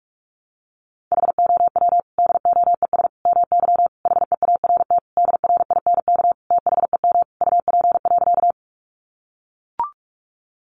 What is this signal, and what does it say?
Morse code, heard as a keyed tone, about 44 wpm, 700 Hz. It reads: HOW DOES MY HEART DRINK THEM UP.
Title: Sentences from Top 1500 Words 45wpm